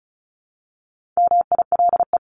Morse code, heard as a keyed tone, about 34 wpm, 700 Hz. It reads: MILE
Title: ES:S Suffix - Top 100 Words with Base Words 35wpm